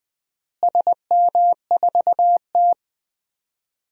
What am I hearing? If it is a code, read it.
Morse code, heard as a keyed tone, about 20 wpm, 700 Hz. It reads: SM4T
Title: DXCC-100 Call Signs - 4 Characters - Repeated 3x - 20wpm